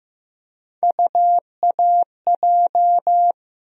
Morse code, heard as a keyed tone, about 15 wpm, 700 Hz. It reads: UAJ
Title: USA Call Sign Postfixes - 3 Letters 15wpm